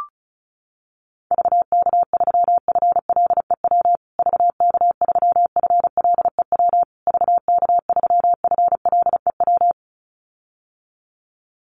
Morse code, heard as a keyed tone, about 34 wpm, 700 Hz. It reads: VK3FLEW VK3FLEW VK3FLEW
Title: USA Call Signs Repeated 3x 35wpm